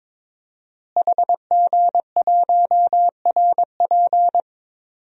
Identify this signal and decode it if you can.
Morse code, heard as a keyed tone, about 22 wpm, 700 Hz. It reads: HG1RP
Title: DXCC-100 Call Signs - 5 Characters - Repeated 3x - 22wpm